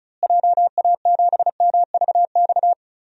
Morse code, heard as a keyed tone, about 34 wpm, 700 Hz. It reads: JA7MVX
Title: DXCC-100 Call Signs - 6 Characters - Repeated 3x - 35wpm